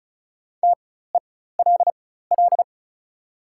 Morse code, heard as a keyed tone, about 36 wpm, spaced as for 16 wpm, 690 Hz. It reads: TELL
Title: 4x Character Spacing - Top 300 Words - 35wpm